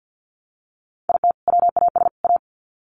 Morse code, heard as a keyed tone, about 50 wpm, 700 Hz. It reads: IT WAS A
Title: Sets of 3 Words 50wpm